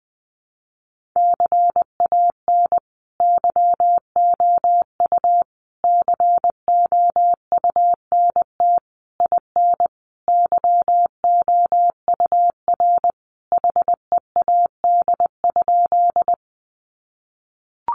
Morse code, heard as a keyed tone, about 20 wpm, 700 Hz. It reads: CAN YOU COUNT IN YOUR HEAD?
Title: Sentences from Top 1500 Words 20wpm